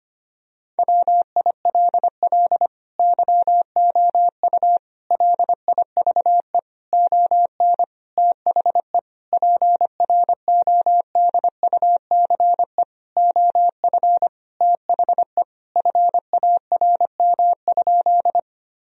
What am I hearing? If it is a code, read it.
Morse code, heard as a keyed tone, about 25 wpm, 700 Hz. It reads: WILL YOU LIVE ON THE PRODUCE OF THE FARM?
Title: Sentences from Top 1000 Words - Review Part 3 - 25wpm